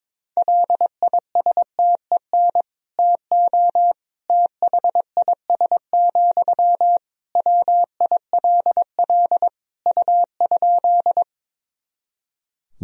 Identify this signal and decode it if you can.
Morse code, heard as a keyed tone, about 22 wpm, 700 Hz. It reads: LISTEN TO THIS, WILL U?
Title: Sentences from Top 400 Words and Abbreviations 22wpm